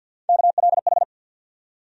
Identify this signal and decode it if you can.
Morse code, heard as a keyed tone, about 51 wpm, 690 Hz. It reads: KLH
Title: USA Call Sign Postfixes - 3 Letters 50wpm